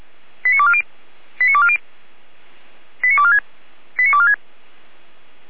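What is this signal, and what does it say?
Five-tone selective call (ZVEI), heard as quick runs of five tones, tones of about 70 ms. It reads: 89240 89240 89246 89246
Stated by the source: Five-tone ZVEI selective call received on an SDR.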